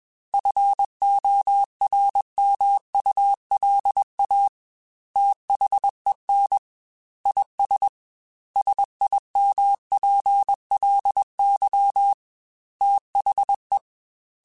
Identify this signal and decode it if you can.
Morse code, heard as a keyed tone, about 21 wpm, 800 Hz. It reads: FORMULA THEN IS SIMPLY THE
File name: Technical Subjects / Technical Subjects-mp3s / XC @20